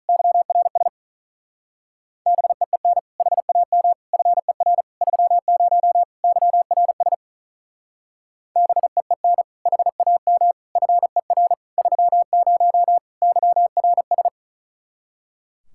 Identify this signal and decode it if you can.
Morse code, heard as a keyed tone, about 43 wpm, 700 Hz. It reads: YRS BEEN HAM FER 30 YRS BEEN HAM FER 30 YRS
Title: Speed-Racing Common QSO Phrases - 35wpm